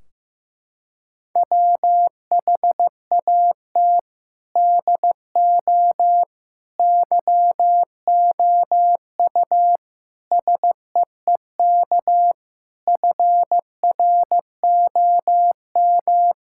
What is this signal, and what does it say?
Morse code, heard as a keyed tone, about 15 wpm, 700 Hz. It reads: WHAT DO YOU SEEK FROM
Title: Sentences from Top 1600 Words 15wpm